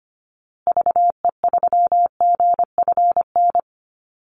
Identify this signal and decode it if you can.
Morse code, heard as a keyed tone, about 25 wpm, 700 Hz. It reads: VE3GFN
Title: Call Signs Repeated 2x - 25wpm